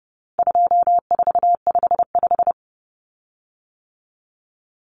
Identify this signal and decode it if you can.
Morse code, heard as a keyed tone, about 30 wpm, 700 Hz. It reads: 2455
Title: Common Cut Numbers - 4-Digit 30wpm